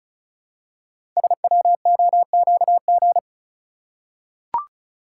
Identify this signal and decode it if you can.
Morse code, heard as a keyed tone, about 36 wpm, 700 Hz. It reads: SWOQG
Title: Code Groups - 5 Letters - 35wpm